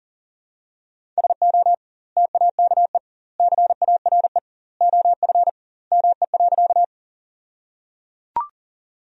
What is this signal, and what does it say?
Morse code, heard as a keyed tone, about 37 wpm, 700 Hz. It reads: SO TAKE CARE OF ME.
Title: 2x Word-Spacing - Sentences from Top 300 Words - 40wpm